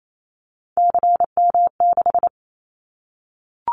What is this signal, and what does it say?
Morse code, heard as a keyed tone, about 28 wpm, 700 Hz. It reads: CM6
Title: Callsign Prefixes for Top 100 DXCC - 28wpm